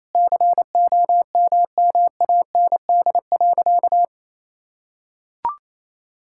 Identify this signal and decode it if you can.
Morse code, heard as a keyed tone, about 28 wpm, 700 Hz. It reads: COMMAND.
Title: Sentences from Top 1600 Words 28wpm